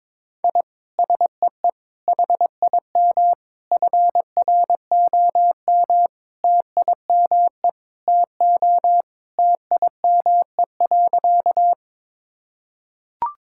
Morse code, heard as a keyed tone, about 21 wpm, 700 Hz. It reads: I SEE HIM FROM TIME TO TIME.
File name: practice / 22wpm / Sentences with 4-Word Phrases - 22wpm